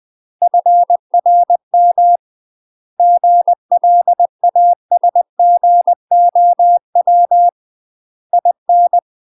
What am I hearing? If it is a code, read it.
Morse code, heard as a keyed tone, about 20 wpm, 700 Hz. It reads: FRM GLASGOW IN
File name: UK MASTER QSOs TRAINING/20WPM